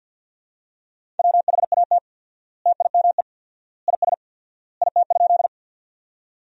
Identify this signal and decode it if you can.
Morse code, heard as a keyed tone, about 51 wpm, 700 Hz. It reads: WHAT TIME IS IT?
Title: 4x Word-Spacing - Sentences from Top 100 Words - 50wpm